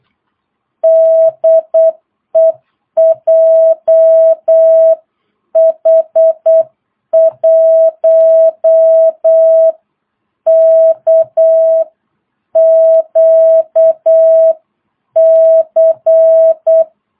Morse code, heard as a keyed tone, about 8 wpm, 650 Hz. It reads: DE JH1KQC
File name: jh8cbh / 2022cw